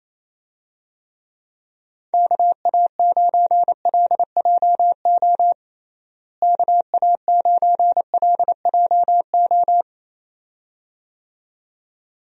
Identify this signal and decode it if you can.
Morse code, heard as a keyed tone, about 28 wpm, 700 Hz. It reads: KA9LJO KA9LJO
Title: USA Call Signs - 6 Characters - Repeated 2x - 28wpm